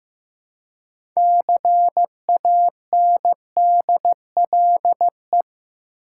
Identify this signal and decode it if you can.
Morse code, heard as a keyed tone, about 15 wpm, 700 Hz. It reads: CANDLE
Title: Six Letter Words - 15wpm